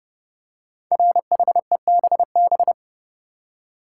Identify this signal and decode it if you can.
Morse code, heard as a keyed tone, about 30 wpm, 700 Hz. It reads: RHEBB
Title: Code Groups - 5 Letters - 30wpm